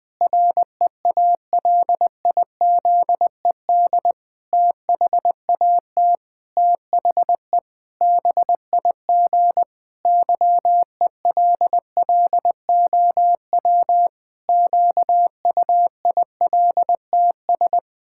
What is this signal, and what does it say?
Morse code, heard as a keyed tone, about 20 wpm, 700 Hz. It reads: REALIZED THAT THE BIG YELLOW QUILTS
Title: Pangram Challenge v2 - 20wpm